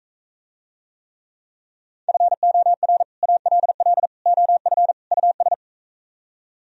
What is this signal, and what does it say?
Morse code, heard as a keyed tone, about 44 wpm, 700 Hz. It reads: FOR ALL OF US